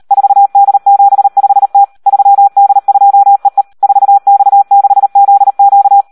Morse code, heard as a keyed tone, about 38 wpm, 800 Hz. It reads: 4D75T 3D2EE 4X6ZQ
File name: Audio practica CW / SD 1 38